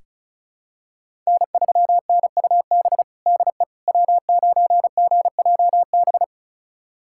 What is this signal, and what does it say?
Morse code, heard as a keyed tone, about 34 wpm, 700 Hz. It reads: N3NUB DE W9GJB
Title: USA Callsign DE Callsign - 5 Characters - 35wpm